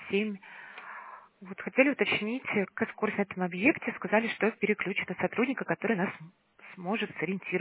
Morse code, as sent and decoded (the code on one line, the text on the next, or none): none